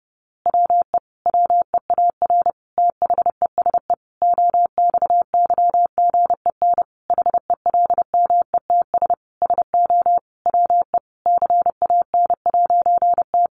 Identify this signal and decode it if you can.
Morse code, heard as a keyed tone, about 28 wpm, 700 Hz. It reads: WE WEAR THESE OXYGEN HELMETS SO WE CAN'T
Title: A Sound Of Thunder 30wpm